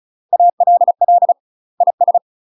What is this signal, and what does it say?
Morse code, heard as a keyed tone, about 36 wpm, 700 Hz. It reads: ALL IS
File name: UK MASTER QSOs TRAINING / 35WPM / UK MASTER QSOs 0036